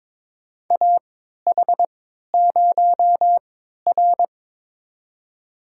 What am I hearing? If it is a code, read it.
Morse code, heard as a keyed tone, about 22 wpm, spaced as for 12 wpm, 700 Hz. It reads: AH0R
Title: USA Call Signs - 4 Characters - 3x Character Spacing - 22wpm